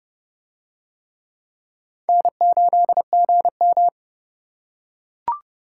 Morse code, heard as a keyed tone, about 30 wpm, 700 Hz. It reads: N8GM